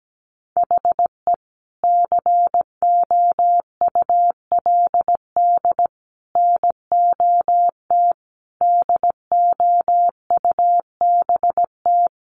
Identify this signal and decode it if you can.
Morse code, heard as a keyed tone, about 17 wpm, 700 Hz. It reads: HE COULD NOT DOUBT
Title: Sentences from Top 1500 Words 17wpm